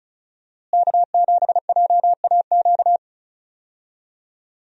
Morse code, heard as a keyed tone, about 36 wpm, 700 Hz. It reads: K7JAQ
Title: USA Call Signs - 5 Characters - Repeated 2x - 35wpm